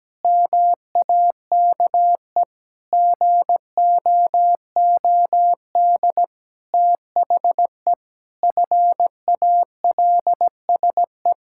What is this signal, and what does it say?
Morse code, heard as a keyed tone, about 17 wpm, 700 Hz. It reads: MAKE GOOD THE FALSE
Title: Sentences from Top 1600 Words 17wpm